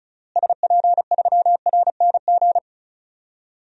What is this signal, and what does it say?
Morse code, heard as a keyed tone, about 34 wpm, 700 Hz. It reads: SP3RNG
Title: DXCC-100 Call Signs - 6 Characters - Repeated 3x - 35wpm